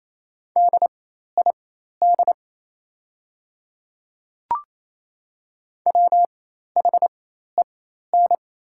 Morse code, heard as a keyed tone, about 28 wpm, spaced as for 13 wpm, 700 Hz. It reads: DID WHEN
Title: 4x Character Spacing - Top 500 Words - 28wpm